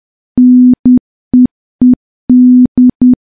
Morse code, heard as a keyed tone, about 10 wpm, 250 Hz. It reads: NEED
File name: morse / morse-training-10WPM